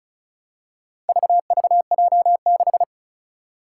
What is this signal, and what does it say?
Morse code, heard as a keyed tone, about 36 wpm, 700 Hz. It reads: VVJ6
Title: Four Challenging Letter-Number Combinations 35wpm